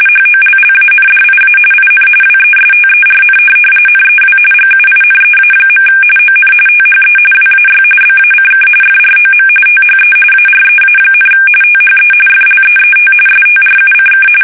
RTTY, about 45.45 baud. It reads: HOLLANDIA IN WESTERN NEW GUINEA. ON THE WAY BACK FROM THE AREA, THE FLEET STRUCK
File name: fsk850 / USS-Massachusetts